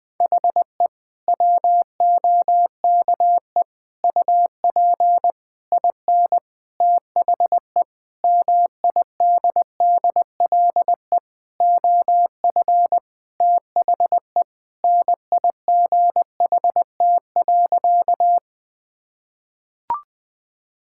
Morse code, heard as a keyed tone, about 20 wpm, 700 Hz. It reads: HE WOKE UP IN THE MIDDLE OF THE NIGHT.
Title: Sentences with 6-Word Phrases - Spoken Beginning - 20wpm